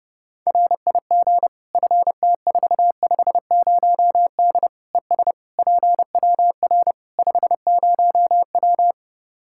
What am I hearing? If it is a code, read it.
Morse code, heard as a keyed tone, about 30 wpm, 700 Hz. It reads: RIG FT450D ES PWR 50W
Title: K1USN SST QSO Practice 30wpm